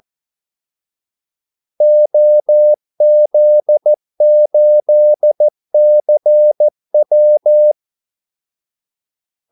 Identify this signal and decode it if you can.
Morse code, heard as a keyed tone, about 14 wpm, 600 Hz. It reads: OZ8CW
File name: Callsigns-14wpm